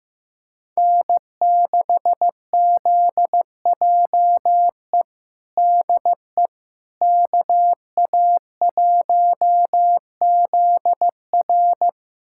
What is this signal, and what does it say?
Morse code, heard as a keyed tone, about 15 wpm, 700 Hz. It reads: N6ZJE DE KA1ZR
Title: USA Callsign DE Callsign - 5 Characters - 15wpm